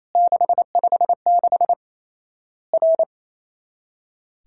Morse code, heard as a keyed tone, about 27 wpm, 700 Hz. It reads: 656 R
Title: Worked All Europe DX Contest - QTC Exchange - 28wpm